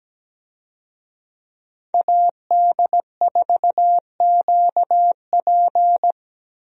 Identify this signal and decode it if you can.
Morse code, heard as a keyed tone, about 17 wpm, 700 Hz. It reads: AD4QP
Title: USA Call Signs - 5 Characters - Repeated 3x - 17wpm